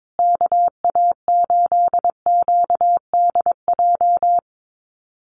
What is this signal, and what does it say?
Morse code, heard as a keyed tone, about 22 wpm, 700 Hz. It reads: KA8QDJ
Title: USA Call Signs Repeated 3x 22wpm